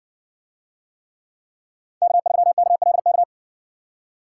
Morse code, heard as a keyed tone, about 60 wpm, 700 Hz. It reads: K3KLK